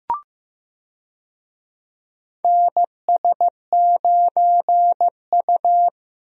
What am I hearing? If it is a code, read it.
Morse code, heard as a keyed tone, about 15 wpm, 700 Hz. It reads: NS9U
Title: USA Call Signs - 4 Characters - Repeated 2x - 15wpm